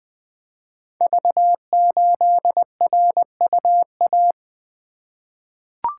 Morse code, heard as a keyed tone, about 20 wpm, 700 Hz. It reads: V8RUA
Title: Code Groups - 5 Letter:Numbers - 20wpm